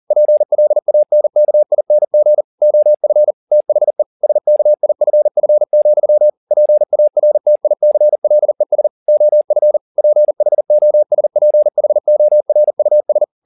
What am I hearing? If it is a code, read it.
Morse code, heard as a keyed tone, about 40 wpm, 600 Hz. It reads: PLANKING OF THE SKIFF, PARTICLES OF PHOSPHORUS
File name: the old man and the sea 2 40wpm